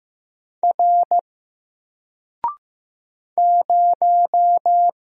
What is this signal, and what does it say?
Morse code, heard as a keyed tone, about 15 wpm, 700 Hz. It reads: R 0
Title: Single Letter-Number Alt 15wpm